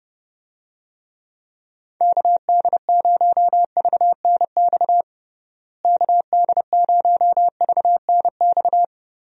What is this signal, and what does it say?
Morse code, heard as a keyed tone, about 30 wpm, 700 Hz. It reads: KD0VNX KD0VNX
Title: USA Call Signs - 6 Characters - Repeated 2x - 30wpm